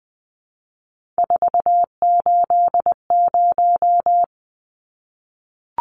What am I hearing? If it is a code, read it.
Morse code, heard as a keyed tone, about 20 wpm, 700 Hz. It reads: 480
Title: Numbers - 3-Digit 20wpm